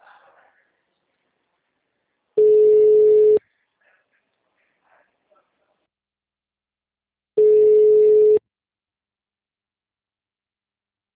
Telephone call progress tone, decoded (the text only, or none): ringback tone